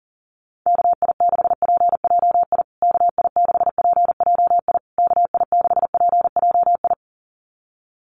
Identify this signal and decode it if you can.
Morse code, heard as a keyed tone, about 40 wpm, 700 Hz. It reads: KI6PJI KI6PJI KI6PJI
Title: USA Call Signs Repeated 3x 40wpm